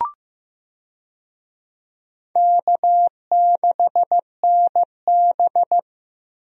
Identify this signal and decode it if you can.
Morse code, heard as a keyed tone, about 15 wpm, 700 Hz. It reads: K6NB
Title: USA Call Signs - 4 Characters - Repeated 2x - 15wpm